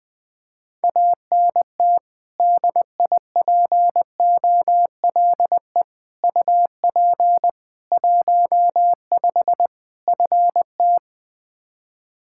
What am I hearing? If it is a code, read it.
Morse code, heard as a keyed tone, about 21 wpm, 700 Hz. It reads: ANT DIPOLE UP 15 FT
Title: Practice QSO - Exchange 2 - Info - 20wpm